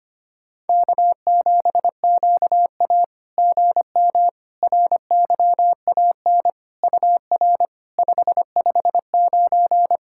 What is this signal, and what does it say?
Morse code, heard as a keyed tone, about 25 wpm, 660 Hz. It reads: K7QA GM RYAN UR 559
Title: SOTA QSO Practice 25wpm